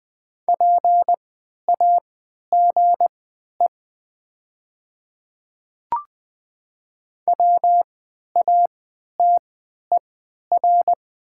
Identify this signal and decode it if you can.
Morse code, heard as a keyed tone, about 20 wpm, spaced as for 11 wpm, 700 Hz. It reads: PAGE WATER
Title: 3x Character Spacing - Top 300 Words - 20wpm